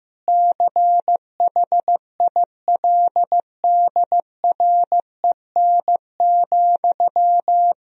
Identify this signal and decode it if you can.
Morse code, heard as a keyed tone, about 15 wpm, 700 Hz. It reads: CHILDREN,
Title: Sentences from Top 1000 Words - Review Part 3 - 15wpm